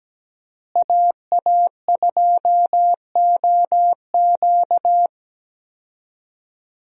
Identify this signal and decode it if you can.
Morse code, heard as a keyed tone, about 17 wpm, 700 Hz. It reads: AA2OQ
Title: USA Call Signs - 5 Characters - Repeated 2x - 17wpm